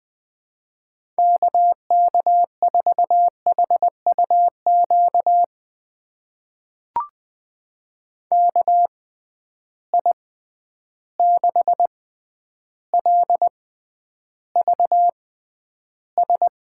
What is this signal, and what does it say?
Morse code, heard as a keyed tone, about 20 wpm, spaced as for 10 wpm, 700 Hz. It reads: KK4HUQ KI6LVS
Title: USA Call Signs - 6 Characters - 6x Character Spacing - 20wpm